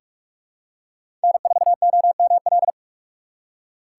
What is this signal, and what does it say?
Morse code, heard as a keyed tone, about 46 wpm, 700 Hz. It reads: N 4 O M L
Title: USA Call Signs - 5 Characters - 6x Character Spacing - 45wpm